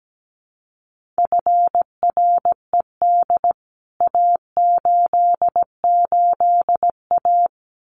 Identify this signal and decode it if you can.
Morse code, heard as a keyed tone, about 17 wpm, 700 Hz. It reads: FRED A88A
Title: CWT Exchange 17wpm